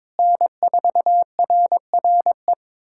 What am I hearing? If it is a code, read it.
Morse code, heard as a keyed tone, about 22 wpm, 700 Hz. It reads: N4RRE
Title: USA Callsign DE Callsign - 5 Characters - 22wpm